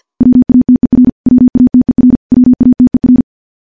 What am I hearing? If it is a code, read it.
Morse code, heard as a keyed tone, about 50 wpm, 250 Hz. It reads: WATER WATER WATER
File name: morse / morse-training-50WPM